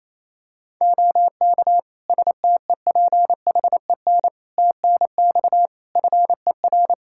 Morse code, heard as a keyed tone, about 28 wpm, 700 Hz. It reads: OK STEPHEN TNX FER
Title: K1USN SST QSO Practice 28wpm